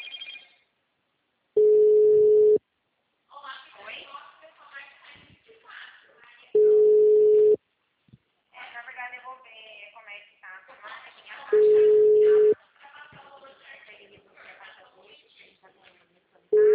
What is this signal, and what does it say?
Ringback tone (Europe, 425 Hz), with speech.